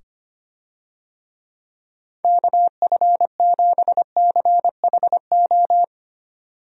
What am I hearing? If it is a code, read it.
Morse code, heard as a keyed tone, about 25 wpm, 700 Hz. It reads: KF7CHO
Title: USA Call Signs - 6 Characters - 25wpm